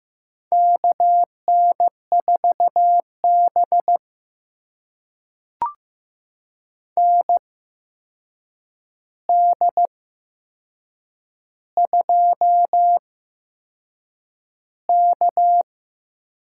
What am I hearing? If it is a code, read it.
Morse code, heard as a keyed tone, about 15 wpm, spaced as for 6 wpm, 700 Hz. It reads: KN4B ND2K
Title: USA Call Signs - 4 Characters - 8x Character Spacing - 15wpm